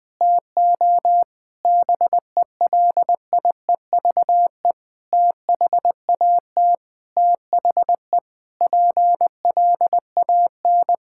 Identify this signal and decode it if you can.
Morse code, heard as a keyed tone, about 20 wpm, 700 Hz. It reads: TO BELIEVE THAT THE PLAN
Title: Sentences with 7-Word Phrases - Spoken Beginning - 20wpm